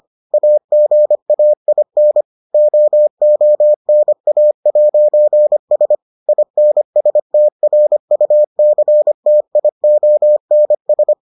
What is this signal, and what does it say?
Morse code, heard as a keyed tone, about 25 wpm, 600 Hz. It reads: AGAIN OONA'S INSTRUCTIONS